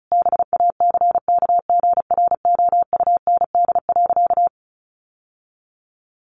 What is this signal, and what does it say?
Morse code, heard as a keyed tone, about 34 wpm, 700 Hz. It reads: BACKGROUND.
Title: Sentences from Top 1300 Words 35wpm